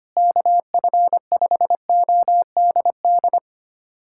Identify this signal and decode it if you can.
Morse code, heard as a keyed tone, about 25 wpm, 700 Hz. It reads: KF5ODD
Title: USA Callsign DE Callsign 25wpm